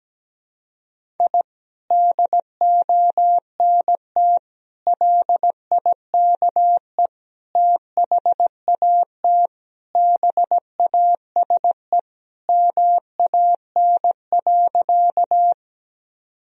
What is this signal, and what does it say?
Morse code, heard as a keyed tone, about 17 wpm, 700 Hz. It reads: I DONT LIKE THAT BASE MAN.
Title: Sentences from Top 1000 Words - Review Part 1 - 17wpm